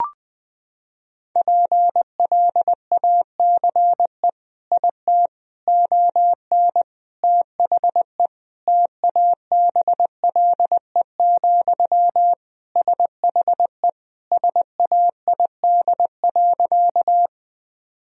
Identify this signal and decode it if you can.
Morse code, heard as a keyed tone, about 20 wpm, 700 Hz. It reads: PLACE IT ON THE TABLE, SHE SAID.
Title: Sentences from Top 1000 Words - Review Part 2 - 20wpm